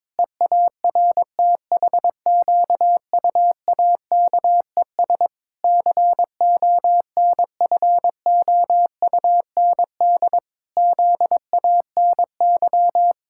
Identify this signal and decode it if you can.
Morse code, heard as a keyed tone, about 22 wpm, 700 Hz. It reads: EARTHQUAKES CONFOUND ZANY
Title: Pangram Challenge v2 - 22wpm